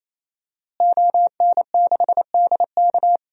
Morse code, heard as a keyed tone, about 27 wpm, 700 Hz. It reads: ON6DK
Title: DXCC-100 Call Signs - 5 Characters - Repeated 3x - 28wpm